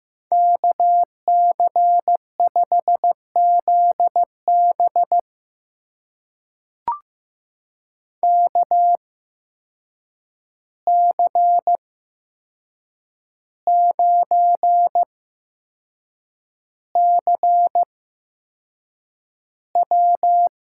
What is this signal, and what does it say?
Morse code, heard as a keyed tone, about 15 wpm, spaced as for 6 wpm, 700 Hz. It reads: KC5ZB KC9CW
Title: USA Call Signs - 5 Characters - 8x Character Spacing - 15wpm